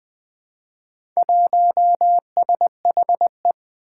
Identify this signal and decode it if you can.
Morse code, heard as a keyed tone, about 20 wpm, 700 Hz. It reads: 1SHE
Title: Four Challenging Letter-Number Combinations 20wpm